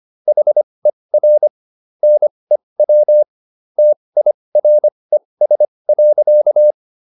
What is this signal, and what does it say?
Morse code, heard as a keyed tone, about 25 wpm, spaced as for 20 wpm, 600 Hz. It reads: HER NEW TIRES.